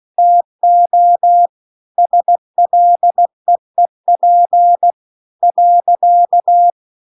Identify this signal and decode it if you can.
Morse code, heard as a keyed tone, about 16 wpm, 700 Hz. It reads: TO SLEEP .